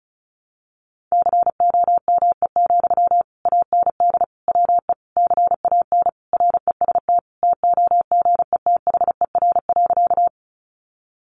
Morse code, heard as a keyed tone, about 34 wpm, 700 Hz. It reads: COME, AND WE CAN REST TOGETHER.
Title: Sentences from Top 500 Words 35wpm